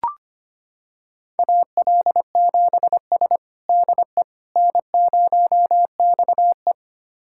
Morse code, heard as a keyed tone, about 25 wpm, 700 Hz. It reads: AL7S DE N0XE
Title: USA Callsign DE Callsign - 4 Characters - 25wpm